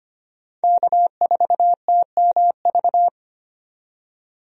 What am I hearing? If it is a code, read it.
Morse code, heard as a keyed tone, about 25 wpm, 700 Hz. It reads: K4TMV
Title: USA Call Signs - 5 Characters - 25wpm